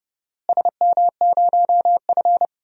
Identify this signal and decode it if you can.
Morse code, heard as a keyed tone, about 31 wpm, 700 Hz. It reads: SM0F
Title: DXCC-100 Call Signs - 4 Characters - Repeated 2x - 30wpm